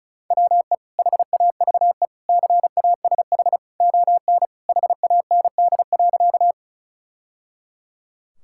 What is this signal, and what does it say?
Morse code, heard as a keyed tone, about 34 wpm, 700 Hz. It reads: WE HAVE CASH ON HAND.
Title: Sentences from Top 1000 Words - Review Part 5 - 35wpm